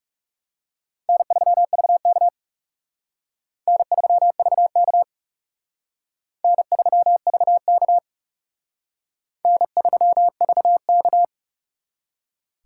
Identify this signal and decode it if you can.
Morse code, heard as a keyed tone, about 36 wpm, 700 Hz. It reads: N3VK N3VK N3VK N3VK
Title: Speed Racing - USA Call Signs 30wpm